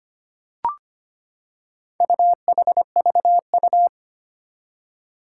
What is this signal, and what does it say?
Morse code, heard as a keyed tone, about 25 wpm, 700 Hz. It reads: UHVU
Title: Four Challenging Letter-Number Combinations 25wpm